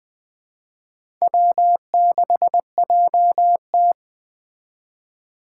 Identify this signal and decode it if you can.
Morse code, heard as a keyed tone, about 20 wpm, 700 Hz. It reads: W6JT